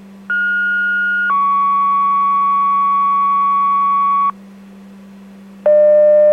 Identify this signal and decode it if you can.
Two-tone page, heard as a one-second tone, then a three-second tone, about 1430 then 1120 Hz. It is heard from 0.3 to 4.3 s.